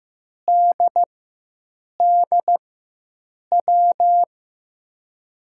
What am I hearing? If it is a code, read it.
Morse code, heard as a keyed tone, about 15 wpm, spaced as for 7 wpm, 700 Hz. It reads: DDW
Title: USA Call Signs - 6 Characters - 4x Character Spacing - 15wpm